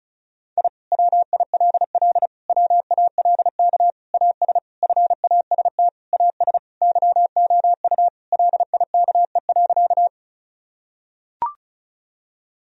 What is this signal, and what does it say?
Morse code, heard as a keyed tone, about 36 wpm, 700 Hz. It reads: I WILL WALK AS FAST AS YOU LIKE.
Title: Sentences from Top 1000 Words - Review Part 2 - 35wpm